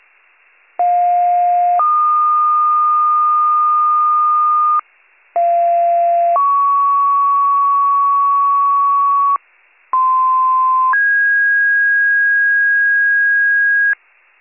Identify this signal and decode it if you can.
Two-tone page, heard as a one-second tone, then a three-second tone, about 710 then 1220 Hz, about 690 then 1120 Hz, about 1040 then 1650 Hz. It is heard from 0.8 to 13.9 s.